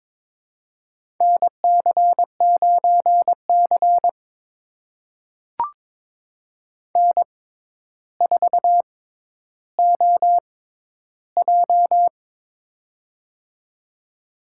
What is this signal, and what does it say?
Morse code, heard as a keyed tone, about 22 wpm, spaced as for 11 wpm, 700 Hz. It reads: NC9C N4OJ
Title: USA Call Signs - 4 Characters - 6x Character Spacing - 22wpm